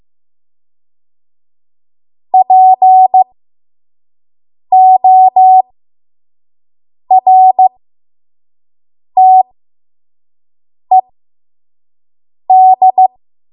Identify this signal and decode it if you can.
Morse code, heard as a keyed tone, about 15 wpm, spaced as for 5 wpm, 750 Hz. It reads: PORTED
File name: Morse / 05WPM